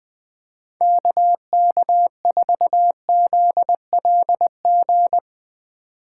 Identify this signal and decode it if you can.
Morse code, heard as a keyed tone, about 22 wpm, 700 Hz. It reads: KK4ZLG
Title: Speed Racing - USA 2x3 Call Signs 20wpm